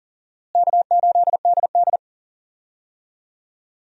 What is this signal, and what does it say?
Morse code, heard as a keyed tone, about 37 wpm, 700 Hz. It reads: K8DD
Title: USA Call Signs - 4 Characters - Repeated 3x - 40wpm